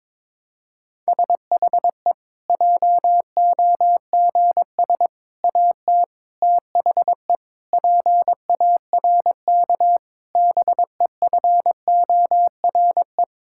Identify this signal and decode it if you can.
Morse code, heard as a keyed tone, about 22 wpm, 700 Hz. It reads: SHE JOGS AT THE PARK BEFORE
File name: Difficult Sentences with Hints - 22wpm